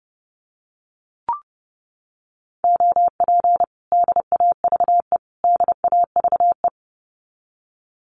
Morse code, heard as a keyed tone, about 30 wpm, 700 Hz. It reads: OP DAVE DAVE
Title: Common QSO Phrases Encore 30wpm